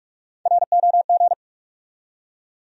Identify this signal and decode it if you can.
Morse code, heard as a keyed tone, about 46 wpm, 700 Hz. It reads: ROG
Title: USA Common Names - Max 5 Letters - 45wpm